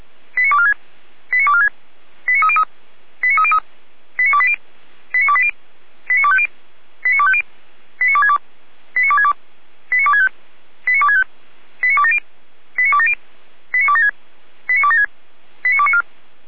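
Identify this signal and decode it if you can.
Five-tone selective call (ZVEI), heard as quick runs of five tones, tones of about 70 ms. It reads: 89246 89246 89303 89303 89280 89280 89250 89250 89262 89262 89265 89265 89270 89270 89276 89276 89294
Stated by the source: ZVEI five-tone call 89240 received on SDR, logged by fire brigade Feuerwehr Orlamünde.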